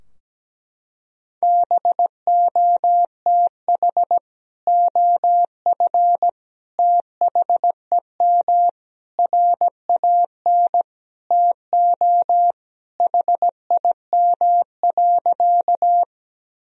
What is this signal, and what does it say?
Morse code, heard as a keyed tone, about 17 wpm, 700 Hz. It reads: BOTH OF THEM RAN TO HIM.
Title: Sentences from Top 1000 Words - Review Part 3 - 17wpm